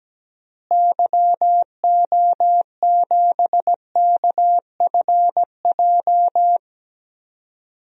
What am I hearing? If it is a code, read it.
Morse code, heard as a keyed tone, about 17 wpm, 700 Hz. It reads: YO7KFJ
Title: DXCC-100 Call Signs - 6 Characters - Repeated 2x - 17wpm